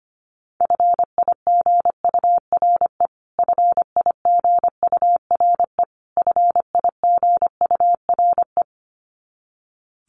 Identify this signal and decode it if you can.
Morse code, heard as a keyed tone, about 25 wpm, 700 Hz. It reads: FIGURE FIGURE FIGURE
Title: Top 500 Words Repeated 25wpm